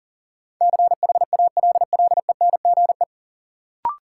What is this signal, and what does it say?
Morse code, heard as a keyed tone, about 40 wpm, 700 Hz. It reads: CHALLENGE
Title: Sentences from Top 1000 Words - Review Part 7 - New Word Drill - 40wpm